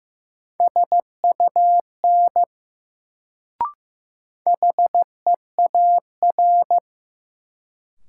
Morse code, heard as a keyed tone, about 16 wpm, 700 Hz. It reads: SUN HEAR
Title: Sentences from Top 300 Words - New Word Drill - 15wpm